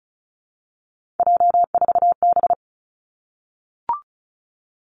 Morse code, heard as a keyed tone, about 36 wpm, 700 Hz. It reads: J4B
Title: Three Challenging Letter-Number Combinations 35wpm